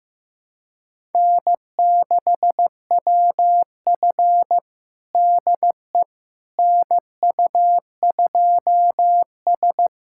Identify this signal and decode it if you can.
Morse code, heard as a keyed tone, about 16 wpm, 700 Hz. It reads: N6WF DE NU2S
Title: USA Callsign DE Callsign - 4 Characters - 15wpm